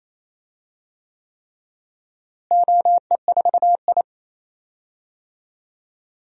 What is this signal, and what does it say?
Morse code, heard as a keyed tone, about 27 wpm, 700 Hz. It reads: OE4I
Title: DXCC-100 Call Signs - 4 Characters - Repeated 3x - 28wpm